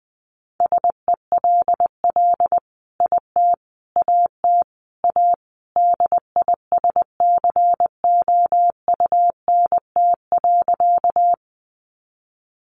Morse code, heard as a keyed tone, about 20 wpm, 700 Hz. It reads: SELL IT AT A DISCOUNT.
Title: Sentences from Top 1200 Words 20wpm